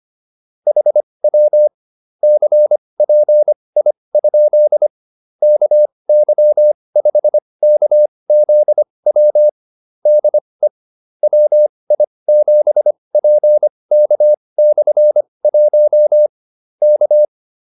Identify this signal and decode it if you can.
Morse code, heard as a keyed tone, about 25 wpm, spaced as for 20 wpm, 600 Hz. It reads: HW CPI? KY5KZW DE WI7PK/1 K
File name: QSO304 20